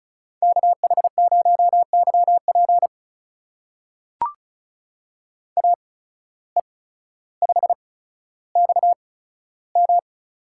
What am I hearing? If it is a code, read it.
Morse code, heard as a keyed tone, about 36 wpm, spaced as for 15 wpm, 700 Hz. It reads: KH0YP AE5XM
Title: USA Call Signs - 5 Characters - 8x Character Spacing - 35wpm